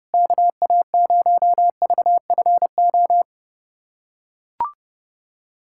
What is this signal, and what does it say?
Morse code, heard as a keyed tone, about 30 wpm, 700 Hz. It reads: KA0VFO
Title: USA Callsign DE Callsign - 6 Characters - 30wpm